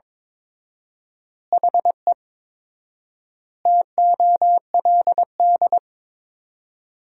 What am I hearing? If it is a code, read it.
Morse code, heard as a keyed tone, about 21 wpm, 700 Hz. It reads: HE TOLD
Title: 4x Word-Spacing - Sentences from Top 500 Words - 22wpm